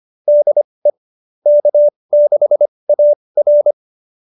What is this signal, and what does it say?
Morse code, heard as a keyed tone, about 25 wpm, spaced as for 20 wpm, 590 Hz. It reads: DE K6AR
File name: QSO307 20